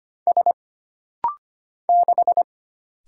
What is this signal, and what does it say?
Morse code, heard as a keyed tone, about 25 wpm, 700 Hz. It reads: S 6
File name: Single Character (Letter:Number:Punctuation:Prosign) - Rapid-Fire - 25wpm